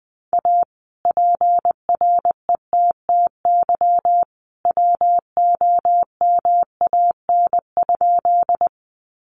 Morse code, heard as a keyed tone, about 20 wpm, 700 Hz. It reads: A PRETTY WOMAN?
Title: Sentences from Top 1100 Words 20wpm